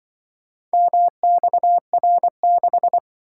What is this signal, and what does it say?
Morse code, heard as a keyed tone, about 25 wpm, 700 Hz. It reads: MXR6
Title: Speed-Racing - Four Letter-Number Combinations - 20wpm